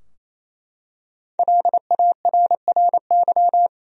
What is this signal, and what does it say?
Morse code, heard as a keyed tone, about 28 wpm, 700 Hz. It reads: LARRY
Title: USA Common Names - 28wpm